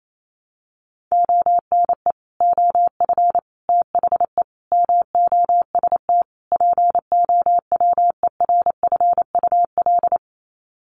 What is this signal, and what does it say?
Morse code, heard as a keyed tone, about 28 wpm, 700 Hz. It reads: ONE OF THE MOST POWERFUL
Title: Sets of 5 Words Encore 28wpm